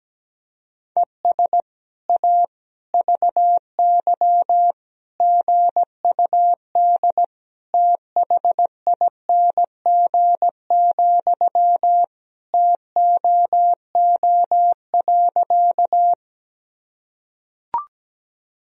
Morse code, heard as a keyed tone, about 17 wpm, 700 Hz. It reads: ES A VY GUD THING, TOO.
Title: Sentences from Top 200 Words and Abbreviations 17wpm